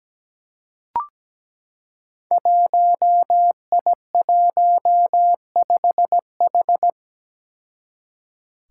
Morse code, heard as a keyed tone, about 17 wpm, 700 Hz. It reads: 1I15H
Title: Five Challenging Letter-Number Combinations 17wpm